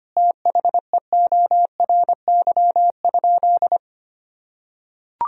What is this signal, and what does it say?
Morse code, heard as a keyed tone, about 25 wpm, 700 Hz. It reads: THEORY?
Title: Sentences from Top 1000 Words - Review Part 3 - 25wpm